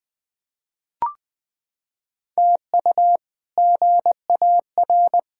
Morse code, heard as a keyed tone, about 21 wpm, 700 Hz. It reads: TU GAR
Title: K1USN SST QSO Practice - 20wpm